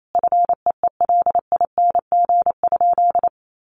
Morse code, heard as a keyed tone, about 28 wpm, 700 Hz. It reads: FEELING?
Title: Sentences from Top 1200 Words 28wpm